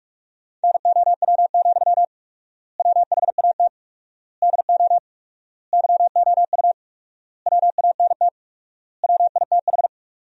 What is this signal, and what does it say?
Morse code, heard as a keyed tone, about 42 wpm, 700 Hz. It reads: NOW, WHAT DO YOU WANT WITH
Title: 4x Word-Spacing - Sentences from Top 200 Words - 45wpm